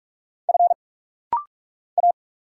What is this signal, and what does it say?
Morse code, heard as a keyed tone, about 44 wpm, 700 Hz. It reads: F A
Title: Single Letters - Rapid-Fire 45wpm